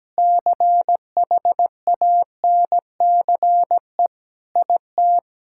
Morse code, heard as a keyed tone, about 17 wpm, 700 Hz. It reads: CHANCE IT
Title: Sentences from Top 1000 Words - Review Part 6 - 17wpm